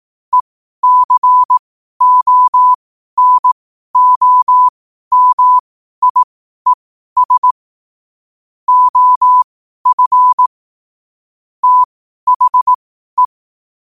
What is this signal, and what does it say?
Morse code, heard as a keyed tone, about 18 wpm, spaced as for 12 wpm, 1000 Hz. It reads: ECONOMIES OF THE
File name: qotdfiles / 12wpm